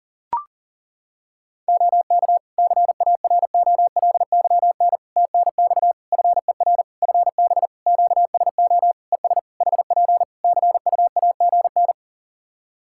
Morse code, heard as a keyed tone, about 37 wpm, 700 Hz. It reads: OK CAROLYN TNX FER FB QSO ES HP CUAGN
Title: Practice QSO - Exchange 4 - Ending - 40wpm